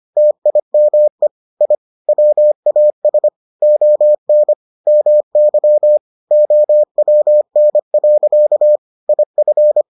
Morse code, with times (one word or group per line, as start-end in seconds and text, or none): 0.2-1.3 TIME
1.6-1.7 I
2.1-3.3 WAS
3.6-4.5 ON
4.9-6.0 MY
6.3-8.8 OWN.
9.1-9.8 IF